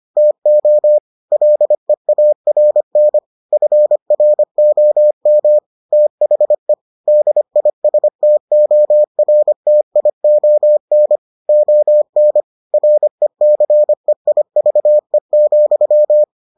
Morse code, heard as a keyed tone, about 25 wpm, 600 Hz. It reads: TO LEARN FROM THE DISTORTION ON RECEIVE,